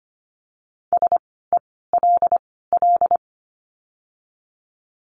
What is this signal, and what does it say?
Morse code, heard as a keyed tone, about 25 wpm, spaced as for 16 wpm, 700 Hz. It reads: SELL
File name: course / 25:10wpm / 094 Lesson 16f - Letter c - Words 25wpm